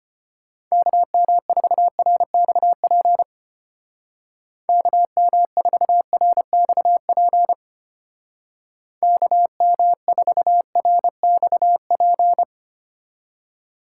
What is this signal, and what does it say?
Morse code, heard as a keyed tone, about 33 wpm, 700 Hz. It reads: KM4RXP KM4RXP KM4RXP